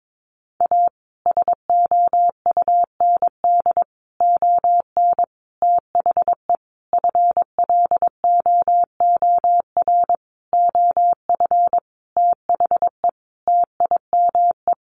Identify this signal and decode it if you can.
Morse code, heard as a keyed tone, about 22 wpm, 700 Hz. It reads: A SOUND ON THE FLOOR OF THE TIME
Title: A Sound Of Thunder 22wpm